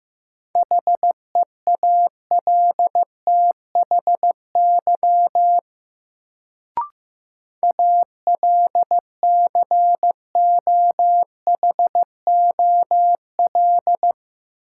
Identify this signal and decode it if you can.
Morse code, heard as a keyed tone, about 15 wpm, 700 Hz. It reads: HEALTHY ALCOHOL
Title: Sentences from Top 1400 Words - New Word Drill Alt - 15wpm